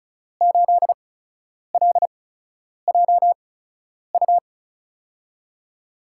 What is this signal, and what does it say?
Morse code, heard as a keyed tone, about 36 wpm, spaced as for 10 wpm, 700 Hz. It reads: 8LJU
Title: USA Call Signs - 8x Character Spacing - 35wpm